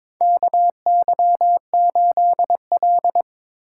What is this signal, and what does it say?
Morse code, heard as a keyed tone, about 22 wpm, 700 Hz. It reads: KY8L
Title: USA Call Signs - 4 Characters - 4x Character Spacing - 22wpm